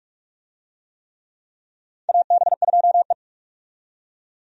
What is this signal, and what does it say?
Morse code, heard as a keyed tone, about 44 wpm, 700 Hz. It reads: AB2E
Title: USA Call Signs - 4 Characters - 45wpm